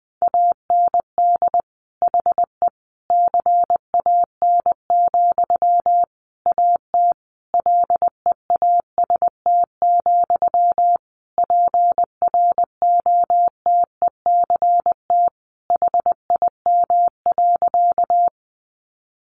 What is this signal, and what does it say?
Morse code, heard as a keyed tone, about 20 wpm, 700 Hz. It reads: AND HE CAN, AT LEAST, PROTECT HIM.
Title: Sentences from Top 900 Words 20wpm